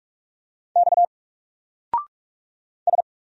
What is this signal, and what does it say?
Morse code, heard as a keyed tone, about 44 wpm, 700 Hz. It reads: X S
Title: Single Character (Letter:Number:Punctuation:Prosign) 45wpm